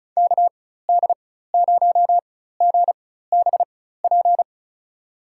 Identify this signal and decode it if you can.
Morse code, heard as a keyed tone, about 36 wpm, spaced as for 16 wpm, 700 Hz. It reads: KD0GBP
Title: USA Call Signs - 6 Characters - 4x Character Spacing - 35wpm